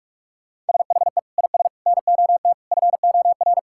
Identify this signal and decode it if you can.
Morse code, heard as a keyed tone, about 46 wpm, 700 Hz. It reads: SHE IS NOT FOR
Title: Sentences from Top 1000 Words - Review Part 5 - 45wpm